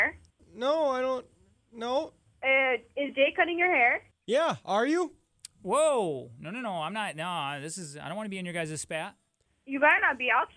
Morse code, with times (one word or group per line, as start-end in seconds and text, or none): none